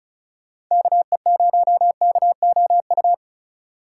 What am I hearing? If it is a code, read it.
Morse code, heard as a keyed tone, about 38 wpm, 700 Hz. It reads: KE0KOU